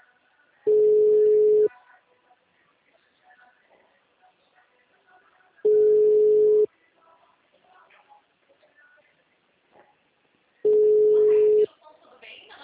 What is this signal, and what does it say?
Ringback tone (Europe, 425 Hz), with speech.